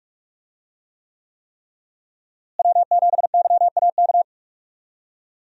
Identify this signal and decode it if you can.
Morse code, heard as a keyed tone, about 41 wpm, 700 Hz. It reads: W7YAK
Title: USA Call Signs - 5 Characters - Repeated 3x - 45wpm